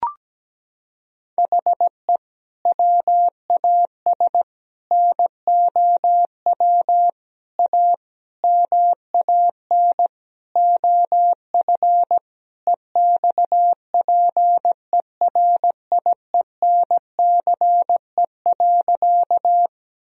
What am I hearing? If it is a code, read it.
Morse code, heard as a keyed tone, about 17 wpm, 700 Hz. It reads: HE WAS NOW A MAN OF EXPERIENCE.
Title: Sentences from Top 1000 Words - Review Part 3 - 17wpm